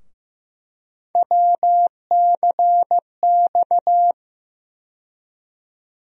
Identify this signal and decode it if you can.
Morse code, heard as a keyed tone, about 15 wpm, 700 Hz. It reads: WCX
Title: USA Call Sign Postfixes - 3 Letters 15wpm